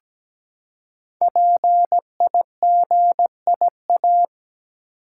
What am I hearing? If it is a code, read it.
Morse code, heard as a keyed tone, about 17 wpm, 700 Hz. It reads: PIGIA
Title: Code Groups - 5 Letters - 17wpm